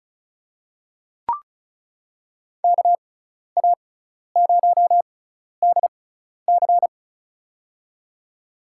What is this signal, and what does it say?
Morse code, heard as a keyed tone, about 36 wpm, spaced as for 12 wpm, 700 Hz. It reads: KA0DC
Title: USA Call Signs - 5 Characters - 6x Character Spacing - 35wpm